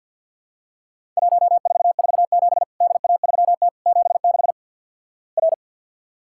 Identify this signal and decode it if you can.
Morse code, heard as a keyed tone, about 50 wpm, 700 Hz. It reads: 1447 DA3T 76 R
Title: Worked All Europe DX Contest - QTC Exchange with Repeats - 50wpm